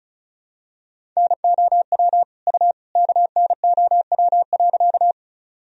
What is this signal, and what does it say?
Morse code, heard as a keyed tone, about 34 wpm, 700 Hz. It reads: NOW U KNOW.
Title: Sentences from Top 100 Words and Abbreviations 35wpm